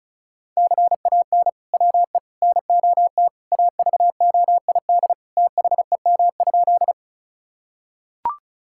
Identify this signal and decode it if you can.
Morse code, heard as a keyed tone, about 36 wpm, 700 Hz. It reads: CAN WE NOT AVOID THEM?
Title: Sentences from Top 1000 Words - Review Part 6 - 35wpm